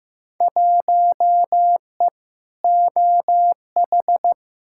Morse code, heard as a keyed tone, about 15 wpm, 700 Hz. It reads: 1E OH
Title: Field Day Exchanges - 15wpm